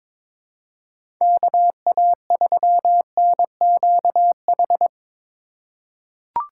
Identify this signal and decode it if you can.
Morse code, heard as a keyed tone, about 23 wpm, 700 Hz. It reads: KA3NQH
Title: USA Call Signs - 6 Characters - 4x Character Spacing - 22wpm